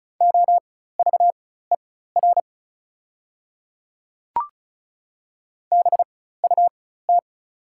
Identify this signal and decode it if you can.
Morse code, heard as a keyed tone, about 34 wpm, spaced as for 16 wpm, 700 Hz. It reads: OVER BUT
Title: 4x Character Spacing - Top 100 Words - 35wpm